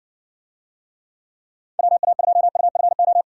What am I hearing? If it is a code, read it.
Morse code, heard as a keyed tone, about 62 wpm, 700 Hz. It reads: WA2UFO